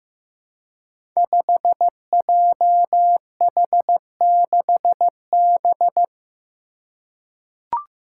Morse code, heard as a keyed tone, about 15 wpm, 700 Hz. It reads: 5JH6B
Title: Five Challenging Letter-Number Combinations 15wpm